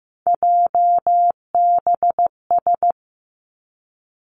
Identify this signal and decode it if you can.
Morse code, heard as a keyed tone, about 15 wpm, 700 Hz. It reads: JBS
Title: Three Challenging Letter-Number Combinations 15wpm